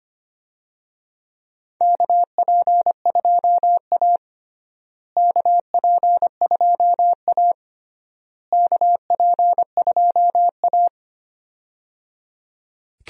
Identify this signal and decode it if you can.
Morse code, heard as a keyed tone, about 25 wpm, 700 Hz. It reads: KP2A KP2A KP2A
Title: DXCC-100 Call Signs - 4 Characters - Repeated 3x - 25wpm